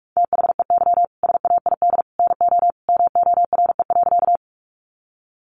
Morse code, heard as a keyed tone, about 44 wpm, 700 Hz. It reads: THEY SAID NO MORE.
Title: Sentences from Top 100 Words 45wpm